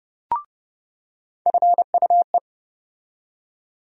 Morse code, heard as a keyed tone, about 30 wpm, 700 Hz. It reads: FUE
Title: USA Call Sign Postfixes - 3 Letters 30wpm